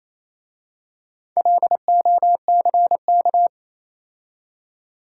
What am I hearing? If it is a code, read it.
Morse code, heard as a keyed tone, about 27 wpm, 700 Hz. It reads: LOCK
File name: Easy Words - Speed-Racing with Hints - 22wpm